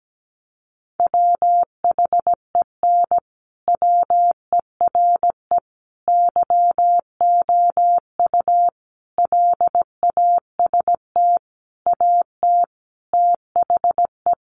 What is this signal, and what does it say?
Morse code, heard as a keyed tone, about 17 wpm, 700 Hz. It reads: WHEN WERE YOU LAST AT THE
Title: Sentences from Top 1300 Words 17wpm